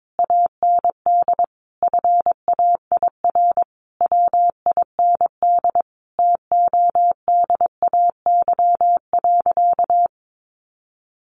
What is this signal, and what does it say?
Morse code, heard as a keyed tone, about 22 wpm, 700 Hz. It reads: AND FAIR WIND TODAY.
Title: Sentences from Top 1400 Words 22wpm